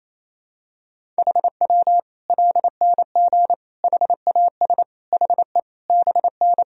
Morse code, heard as a keyed tone, about 28 wpm, 700 Hz. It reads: HW LNG HAS HE BN
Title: Sentences from Top 100 Words and Abbreviations 28wpm